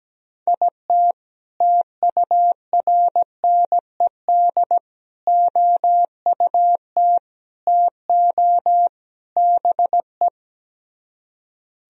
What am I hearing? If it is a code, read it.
Morse code, heard as a keyed tone, about 17 wpm, 700 Hz. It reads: IT TURNED OUT TO BE
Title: Speed-Racing - Sets of 5 Words 17wpm